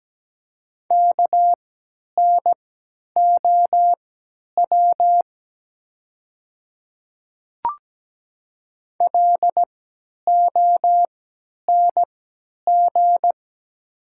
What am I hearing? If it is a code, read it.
Morse code, heard as a keyed tone, about 17 wpm, spaced as for 10 wpm, 700 Hz. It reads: KNOW LONG
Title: 3x Character Spacing - Top 100 Words - 17wpm